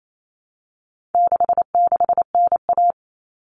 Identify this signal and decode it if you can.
Morse code, heard as a keyed tone, about 28 wpm, 700 Hz. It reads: 66NA
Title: All Cut Numbers - 4-Digit 28wpm